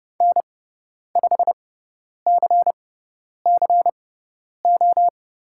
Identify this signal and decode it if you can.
Morse code, heard as a keyed tone, about 30 wpm, spaced as for 10 wpm, 700 Hz. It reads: N5CCO
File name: Lesson 18g - Letter m - Call Signs 30wpm